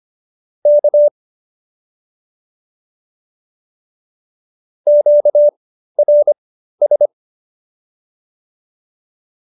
Morse code, heard as a keyed tone, about 25 wpm, spaced as for 13 wpm, 590 Hz. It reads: K QRS